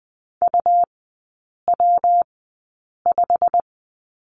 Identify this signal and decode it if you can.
Morse code, heard as a keyed tone, about 19 wpm, 700 Hz. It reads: U W 5
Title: 114 Lesson 19e - Letter w - Three Characters 20wpm